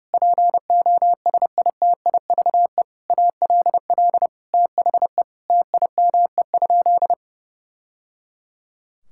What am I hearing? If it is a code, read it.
Morse code, heard as a keyed tone, about 30 wpm, 700 Hz. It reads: POSITIVE ALL THE TIME?
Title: Sentences from Top 1000 Words - Review Part 5 - 30wpm